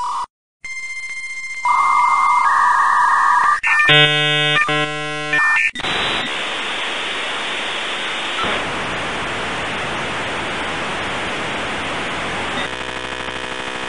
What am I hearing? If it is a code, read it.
Dial-up modem handshake.
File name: le dita nella presa / music / Suono Modem 56K-JpH2YFzP9WY